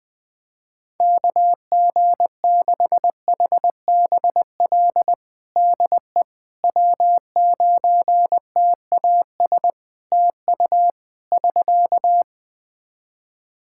Morse code, heard as a keyed tone, about 20 wpm, 700 Hz. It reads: KG6HBL DE W9TAS TU <SK>
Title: Practice QSO - Exchange 4 - Ending - 20wpm